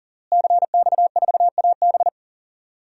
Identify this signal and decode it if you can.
Morse code, heard as a keyed tone, about 38 wpm, 700 Hz. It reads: CX4AB
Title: DXCC-100 Call Signs - 5 Characters - Repeated 3x - 40wpm